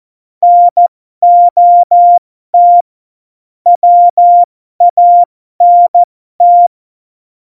Morse code, heard as a keyed tone, about 14 wpm, spaced as for 12 wpm, 700 Hz. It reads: NOT WANT
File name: Morsekurs / Witze